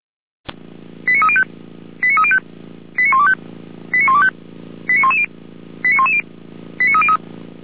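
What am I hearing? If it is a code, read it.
Five-tone selective call (ZVEI), heard as quick runs of five tones, tones of about 70 ms. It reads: 89305 89305 89125 89125 89110 89110 89303
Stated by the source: ZVEI five-tone call 89276 received by SDR, fire brigade Orlamünde (feuerwehr-orlamuende).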